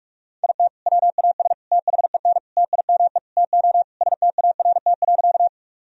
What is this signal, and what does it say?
Morse code, heard as a keyed tone, about 44 wpm, 700 Hz. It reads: IT WAS THEN TIME TO START.
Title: Sentences from Top 1000 Words - Review Part 1 - 45wpm